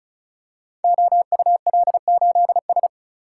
Morse code, heard as a keyed tone, about 34 wpm, 700 Hz. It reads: OUL8S
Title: Code Groups - 5 Letter:Numbers - 35wpm